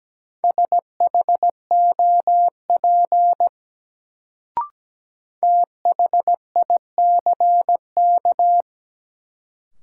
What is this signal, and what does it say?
Morse code, heard as a keyed tone, about 17 wpm, 700 Hz. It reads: SHOP THICK
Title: Sentences from Top 1600 Words - New Word Drill - 17wpm